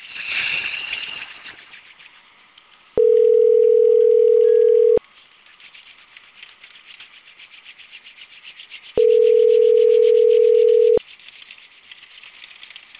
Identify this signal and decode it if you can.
Ringback tone (North America), 440 + 480 Hz.